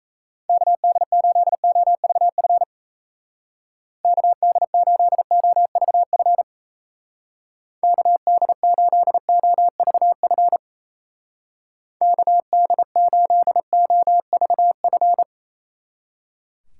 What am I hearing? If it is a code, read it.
Morse code, heard as a keyed tone, about 35 wpm, 700 Hz. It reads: KD8OVF KD8OVF KD8OVF KD8OVF
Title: Speed Racing - USA 2x3 Call Signs 28wpm